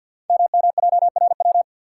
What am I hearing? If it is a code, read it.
Morse code, heard as a keyed tone, about 50 wpm, 690 Hz. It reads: MMJRW
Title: Code Groups - 5 Letters - 50wpm